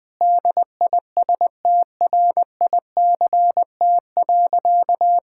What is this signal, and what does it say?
Morse code, heard as a keyed tone, about 21 wpm, 700 Hz. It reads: DISTRICT.
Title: Sentences from Top 1600 Words 20wpm